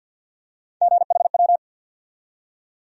Morse code, heard as a keyed tone, about 53 wpm, 700 Hz. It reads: GHW
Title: USA Call Sign Postfixes - 3 Letters 50wpm